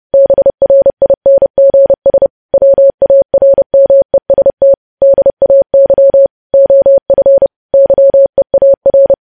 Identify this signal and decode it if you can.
Morse code, heard as a keyed tone, about 30 wpm, 550 Hz. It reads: BRINGS WARMEST DAY OF YEAR
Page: news headlines in morse code at 30 wpm